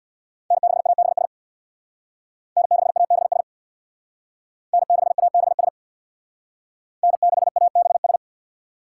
Morse code, heard as a keyed tone, about 62 wpm, 700 Hz. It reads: N6ABS N6ABS N6ABS N6ABS
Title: Speed Racing - USA 1x3 Call Signs 50wpm